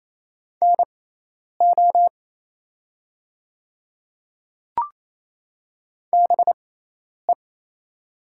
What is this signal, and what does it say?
Morse code, heard as a keyed tone, about 28 wpm, spaced as for 10 wpm, 700 Hz. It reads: NO BE
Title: 6x Character Spacing - Top 200 Words - 28wpm